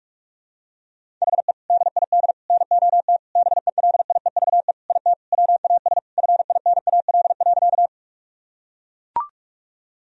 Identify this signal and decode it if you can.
Morse code, heard as a keyed tone, about 41 wpm, 700 Hz. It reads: HE DID NOT BELIEVE IT WAS FINAL.
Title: Sentences from Top 1000 Words - Review Part 2 - 45wpm